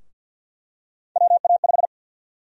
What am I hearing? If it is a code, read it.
Morse code, heard as a keyed tone, about 51 wpm, 700 Hz. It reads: WA5
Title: USA Call Sign Prefixes - 50wpm